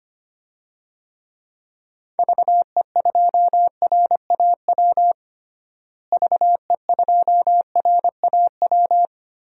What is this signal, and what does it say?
Morse code, heard as a keyed tone, about 25 wpm, 700 Hz. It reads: VE2RAW VE2RAW
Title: DXCC-100 Call Signs - 6 Characters - Repeated 3x - 25wpm